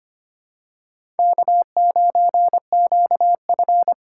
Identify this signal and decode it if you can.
Morse code, heard as a keyed tone, about 25 wpm, 700 Hz. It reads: K9QF
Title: USA Call Signs - 2x Character Spacing - 25wpm